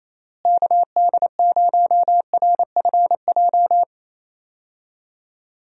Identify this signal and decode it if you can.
Morse code, heard as a keyed tone, about 28 wpm, 700 Hz. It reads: KD0RFJ
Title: USA Call Signs - 6 Characters - Repeated 3x - 28wpm